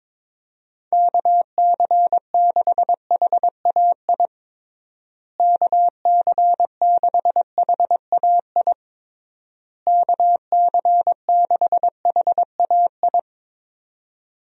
Morse code, heard as a keyed tone, about 22 wpm, 700 Hz. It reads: KC6HAI KC6HAI KC6HAI
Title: USA Call Signs - 6 Characters - Repeated 3x - 22wpm